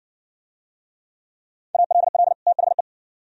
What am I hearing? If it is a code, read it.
Morse code, heard as a keyed tone, about 56 wpm, 700 Hz. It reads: ALL THE